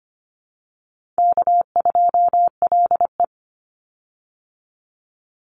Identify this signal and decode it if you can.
Morse code, heard as a keyed tone, about 25 wpm, 700 Hz. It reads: K2LE
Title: Easy Call Signs - 25wpm